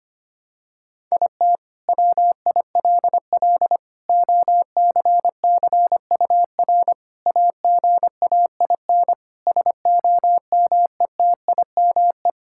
Morse code, heard as a keyed tone, about 25 wpm, 700 Hz. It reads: IT WILL OCCUR AGAIN SOMETIME
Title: Sentences from Top 1000 Words - Review Part 6 - 25wpm